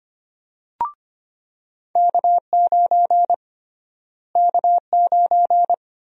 Morse code, heard as a keyed tone, about 25 wpm, 700 Hz. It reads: K9 K9
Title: USA Call Sign Prefixes - Repeated 2x - 25wpm